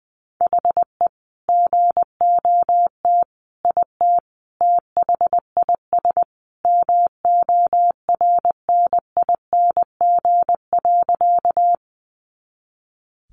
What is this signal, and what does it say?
Morse code, heard as a keyed tone, about 20 wpm, 700 Hz. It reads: HE GOT IT THIS MORNING.
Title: Sentences from Top 400 Words 20wpm